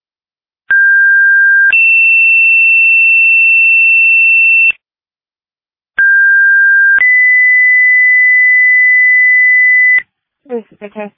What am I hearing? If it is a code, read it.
Two-tone page, heard as a one-second tone, then a three-second tone, about 1600 then 2700 Hz, about 1600 then 2000 Hz, with speech. It is heard from 0.7 to 10.0 s.